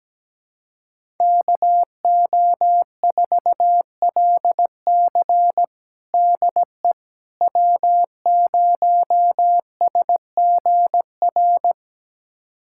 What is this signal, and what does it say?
Morse code, heard as a keyed tone, about 17 wpm, 700 Hz. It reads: KO4LC DE W0SGR
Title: USA Callsign DE Callsign - 5 Characters - 17wpm